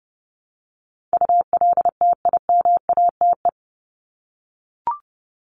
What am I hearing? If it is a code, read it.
Morse code, heard as a keyed tone, about 30 wpm, 700 Hz. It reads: ULTIMATE
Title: LY Suffix - Top 100 Words with Base Words 30wpm